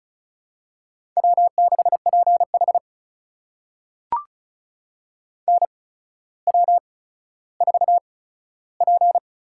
Morse code, heard as a keyed tone, about 36 wpm, spaced as for 15 wpm, 700 Hz. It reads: W6PH NW4P
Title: USA Call Signs - 4 Characters - 8x Character Spacing - 35wpm